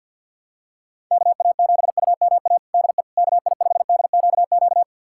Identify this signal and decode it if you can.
Morse code, heard as a keyed tone, about 51 wpm, 700 Hz. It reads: KA7UMA DE KI5DQQ
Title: USA Callsign DE Callsign - 6 Characters - 50wpm